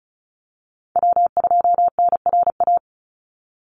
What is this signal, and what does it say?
Morse code, heard as a keyed tone, about 34 wpm, 700 Hz. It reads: W2NRA
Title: Call Signs Repeated 2x - 35wpm